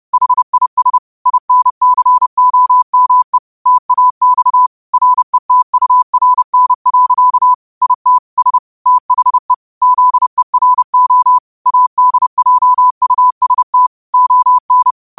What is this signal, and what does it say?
Morse code, heard as a keyed tone, about 30 wpm, 1000 Hz. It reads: HIS INCOME TAX RETURN. IT S THE ZERO ADJUST ON
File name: qotdfiles / 30wpm